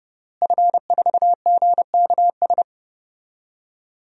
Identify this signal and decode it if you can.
Morse code, heard as a keyed tone, about 30 wpm, 700 Hz. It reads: F4GKS
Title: DXCC-100 Call Signs - 5 Characters - Repeated 2x - 30wpm